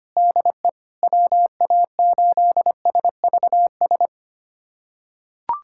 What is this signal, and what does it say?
Morse code, heard as a keyed tone, about 25 wpm, 700 Hz. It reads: DE WA8SVS
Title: USA Callsign DE Callsign 25wpm